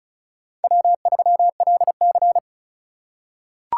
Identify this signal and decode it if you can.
Morse code, heard as a keyed tone, about 34 wpm, 700 Hz. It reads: W3LC
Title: USA Call Signs - 4 Characters - 4x Character Spacing - 35wpm